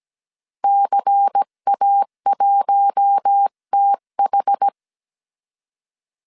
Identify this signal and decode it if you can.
Morse code, heard as a keyed tone, about 17 wpm, 800 Hz. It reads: CA1TH
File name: CW / DET / 1 6 1500C S C 16wpm 2733